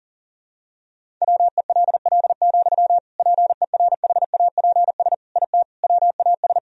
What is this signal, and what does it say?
Morse code, heard as a keyed tone, about 40 wpm, 700 Hz. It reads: WELL, PERHAPS IT WAS
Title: Sentences from Top 500 Words and Abbreviations 40wpm